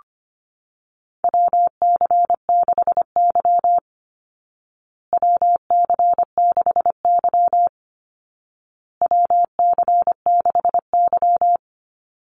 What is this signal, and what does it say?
Morse code, heard as a keyed tone, about 25 wpm, 700 Hz. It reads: WC6Y WC6Y WC6Y
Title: Call Signs Repeated 2x - 25wpm